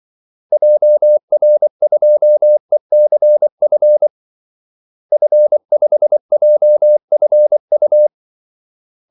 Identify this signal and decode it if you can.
Morse code, heard as a keyed tone, about 24 wpm, 600 Hz. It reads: JR2ECF F5JFU
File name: Callsigns-24wpm